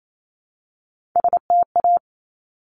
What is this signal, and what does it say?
Morse code, heard as a keyed tone, about 28 wpm, 700 Hz. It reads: STA
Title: Top Three Letter Combinations in Words 28wpm